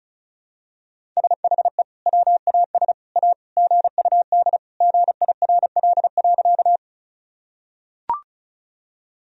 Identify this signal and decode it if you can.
Morse code, heard as a keyed tone, about 34 wpm, 700 Hz. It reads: SHE WAS A GUD GIRL.
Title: Sentences from Top 400 Words and Abbreviations 35wpm